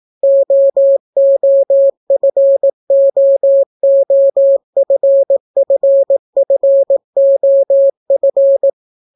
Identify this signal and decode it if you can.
Morse code, heard as a keyed tone, about 18 wpm, 550 Hz. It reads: OOFOOFFFOF